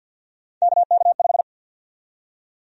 Morse code, heard as a keyed tone, about 52 wpm, 700 Hz. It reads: KK5
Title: USA Call Sign Prefixes - Repeated 2x - 50wpm